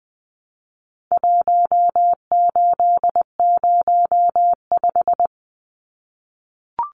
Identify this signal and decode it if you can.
Morse code, heard as a keyed tone, about 20 wpm, 700 Hz. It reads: 1805
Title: Numbers - Contest Style - 4-Digit 20wpm